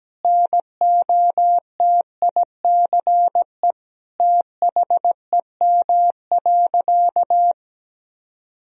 Morse code, heard as a keyed tone, about 17 wpm, 700 Hz. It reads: NOTICE THEM.
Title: Sentences from Top 1000 Words - Review Part 2 - 17wpm